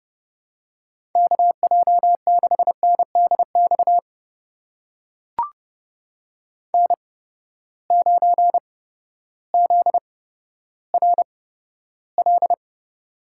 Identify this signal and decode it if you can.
Morse code, heard as a keyed tone, about 30 wpm, spaced as for 14 wpm, 700 Hz. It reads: KJ6NDX N9ZRL
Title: USA Call Signs - 8x Character Spacing - 30wpm